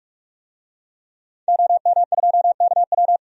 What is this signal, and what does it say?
Morse code, heard as a keyed tone, about 46 wpm, 700 Hz. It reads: OM2KW
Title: DXCC-100 Call Signs - 5 Characters - 45wpm